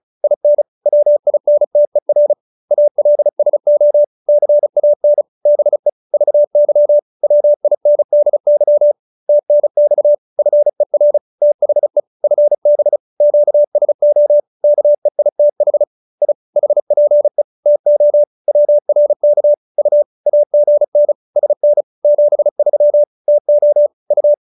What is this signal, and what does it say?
Morse code, heard as a keyed tone, about 34 wpm, 590 Hz. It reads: IN WINTER ALSO CAN BE VY WINDY TNX FER THE FB QSO KEITH I HPE TO WRK U AGN SN 73 TO U